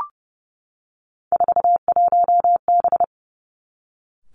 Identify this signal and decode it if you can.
Morse code, heard as a keyed tone, about 30 wpm, 700 Hz. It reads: 41B
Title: Three Challenging Letter-Number Combinations 30wpm